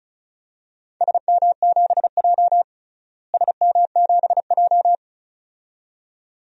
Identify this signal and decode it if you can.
Morse code, heard as a keyed tone, about 34 wpm, 700 Hz. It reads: SM7J SM7J
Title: DXCC-100 Call Signs - 4 Characters - Repeated 2x - 35wpm